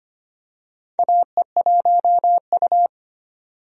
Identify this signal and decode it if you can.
Morse code, heard as a keyed tone, about 25 wpm, 700 Hz. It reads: AE1U
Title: USA Call Signs - 4 Characters - 25wpm